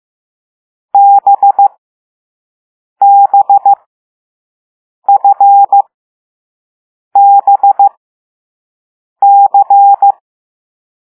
Morse code, heard as a keyed tone, about 15 wpm, spaced as for 5 wpm, 800 Hz. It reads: BBFBC
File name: Morse / Level6B